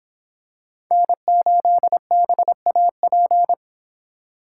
Morse code, heard as a keyed tone, about 25 wpm, 700 Hz. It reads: N8BAP